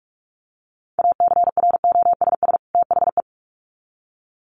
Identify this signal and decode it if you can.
Morse code, heard as a keyed tone, about 46 wpm, 700 Hz. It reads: ACROSS THE
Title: Sets of 2 Words 45wpm